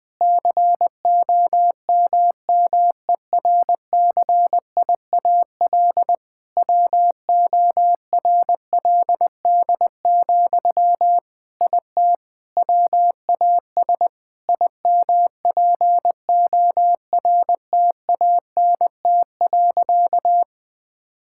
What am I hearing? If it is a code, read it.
Morse code, heard as a keyed tone, about 21 wpm, 700 Hz. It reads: COMMERCIAL WORLD, IT WAS IMPORTANT.
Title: Sentences from Top 1000 Words - Review Part 5 - 20wpm